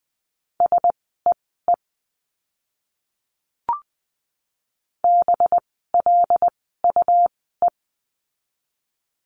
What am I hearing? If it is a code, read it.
Morse code, heard as a keyed tone, about 20 wpm, spaced as for 14 wpm, 700 Hz. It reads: SEE BLUE
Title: 185 Lesson 30f - Letter b - Words 20wpm